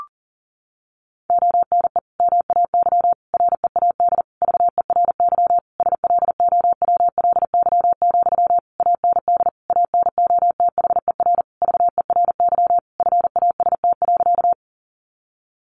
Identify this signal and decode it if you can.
Morse code, heard as a keyed tone, about 40 wpm, 700 Hz. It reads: ONE MAY READ VERY SLOWLY, AND ANOTHER VERY FAST.
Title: Sentences from Top 1100 Words 40wpm